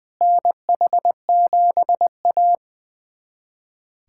Morse code, heard as a keyed tone, about 21 wpm, 700 Hz. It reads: NH7A
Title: USA Callsign DE Callsign - 4 Characters - 20wpm